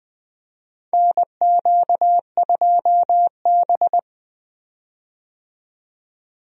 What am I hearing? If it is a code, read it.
Morse code, heard as a keyed tone, about 20 wpm, 700 Hz. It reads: NQ2B
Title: USA Call Signs - 4 Characters - 20wpm